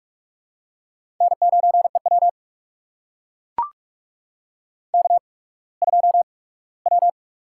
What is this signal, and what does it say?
Morse code, heard as a keyed tone, about 44 wpm, spaced as for 21 wpm, 700 Hz. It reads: N9EW K2W
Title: USA Call Signs - 4 Characters - 8x Character Spacing - 45wpm